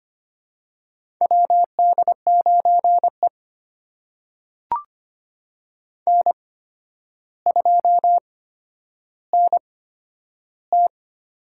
Morse code, heard as a keyed tone, about 25 wpm, spaced as for 11 wpm, 700 Hz. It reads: WD9E N2NT
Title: USA Call Signs - 4 Characters - 8x Character Spacing - 25wpm